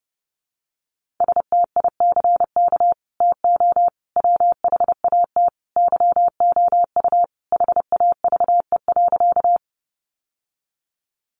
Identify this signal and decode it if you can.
Morse code, heard as a keyed tone, about 30 wpm, 700 Hz. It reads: STICK TO WHAT YOU HAVE.
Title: Sentences from Top 1200 Words 30wpm